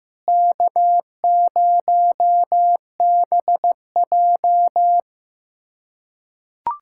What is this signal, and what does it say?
Morse code, heard as a keyed tone, about 16 wpm, 700 Hz. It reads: K0BJ
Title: USA Callsign DE Callsign - 4 Characters - 15wpm